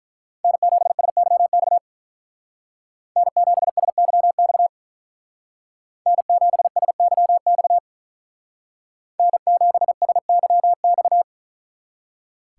Morse code, heard as a keyed tone, about 44 wpm, 700 Hz. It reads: N7SYX N7SYX N7SYX N7SYX
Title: Speed Racing - USA 1x3 Call Signs 35wpm